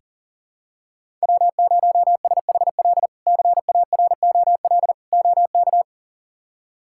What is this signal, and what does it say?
Morse code, heard as a keyed tone, about 37 wpm, 700 Hz. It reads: W0SHL CAROL OK
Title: K1USN SST QSO Practice - 40wpm